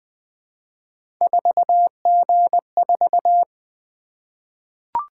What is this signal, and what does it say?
Morse code, heard as a keyed tone, about 21 wpm, 700 Hz. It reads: 4G4
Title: Three Letter-Number Combinations - 20wpm